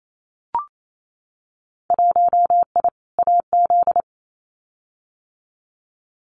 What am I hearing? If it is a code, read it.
Morse code, heard as a keyed tone, about 28 wpm, 700 Hz. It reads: 1I AZ
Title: Winter Field Day Exchanges 28wpm